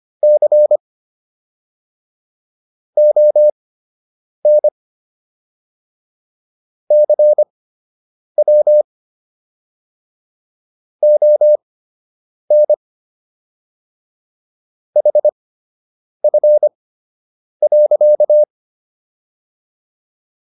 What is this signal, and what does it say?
Morse code, heard as a keyed tone, about 25 wpm, spaced as for 8 wpm, 600 Hz. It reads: C ON CW ON HF.